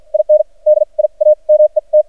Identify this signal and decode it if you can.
Morse code, heard as a keyed tone, about 53 wpm, spaced as for 36 wpm, 610 Hz. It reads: INDIA MET
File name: archived / CRSnet